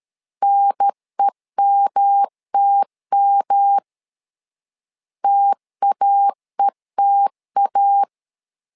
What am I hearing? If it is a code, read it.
Morse code, heard as a keyed tone, about 13 wpm, 800 Hz. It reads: NEMTM TAETA
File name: LEZIONI / 3 MWRO / 3 M 12wpm 1541